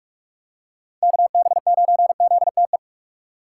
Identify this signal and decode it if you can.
Morse code, heard as a keyed tone, about 46 wpm, 700 Hz. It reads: KB9ZTE